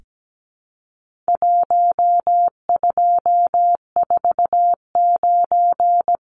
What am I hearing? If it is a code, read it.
Morse code, heard as a keyed tone, about 17 wpm, 700 Hz. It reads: 1249
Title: Numbers - 4-Digit 17wpm